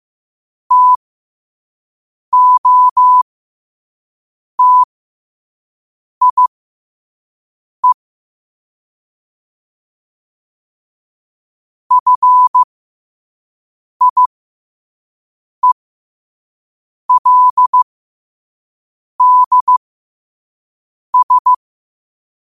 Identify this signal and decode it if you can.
Morse code, heard as a keyed tone, about 15 wpm, spaced as for 5 wpm, 1000 Hz. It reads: TOTIE FIELDS
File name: qotdfiles / 5wpm